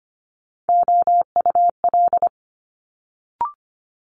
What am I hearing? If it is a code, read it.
Morse code, heard as a keyed tone, about 25 wpm, 700 Hz. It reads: OUL
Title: Multi-letter Phonemes Quick Fire 25wpm